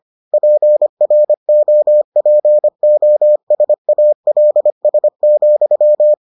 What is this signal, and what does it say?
Morse code, heard as a keyed tone, about 25 wpm, 600 Hz. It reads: PROPOSALS,